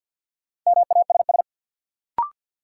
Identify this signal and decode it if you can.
Morse code, heard as a keyed tone, about 53 wpm, 700 Hz. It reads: MASS
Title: Sets of 5 Words Encore - New Word Drill - 50wpm